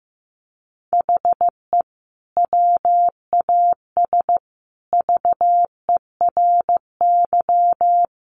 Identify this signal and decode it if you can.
Morse code, heard as a keyed tone, about 15 wpm, 700 Hz. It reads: HE WAS VERY
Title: Sentences from Top 1400 Words 15wpm